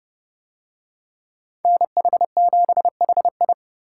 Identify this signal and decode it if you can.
Morse code, heard as a keyed tone, about 29 wpm, 700 Hz. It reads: NH7HI